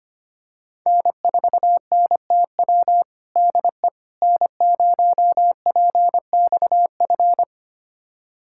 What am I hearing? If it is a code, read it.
Morse code, heard as a keyed tone, about 25 wpm, 700 Hz. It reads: N4NTW DE N0PXF
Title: USA Callsign DE Callsign - 5 Characters - 25wpm